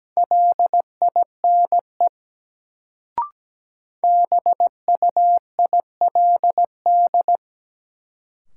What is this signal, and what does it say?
Morse code, heard as a keyed tone, about 17 wpm, 700 Hz. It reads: LINE BUILD
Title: Sentences from Top 1000 Words - Review Part 1 - New Word Drill - 17wpm